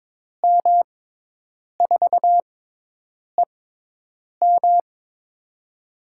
Morse code, heard as a keyed tone, about 22 wpm, spaced as for 8 wpm, 700 Hz. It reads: M4EM
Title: USA Call Signs - 5 Characters - 6x Character Spacing - 22wpm